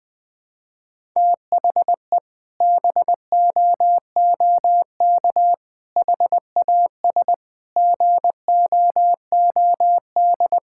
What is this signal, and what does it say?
Morse code, heard as a keyed tone, about 20 wpm, 700 Hz. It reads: THE BOOK HAS GOOD
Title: Sentences from Top 1000 Words - Review Part 6 - 20wpm